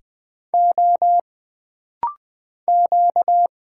Morse code, heard as a keyed tone, about 20 wpm, 700 Hz. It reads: O Q
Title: Single Letter-Number Alt - Rapid-Fire 20wpm